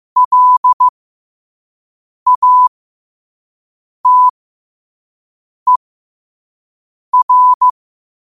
Morse code, heard as a keyed tone, about 15 wpm, spaced as for 5 wpm, 1000 Hz. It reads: LATER
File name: qotdfiles / 5wpm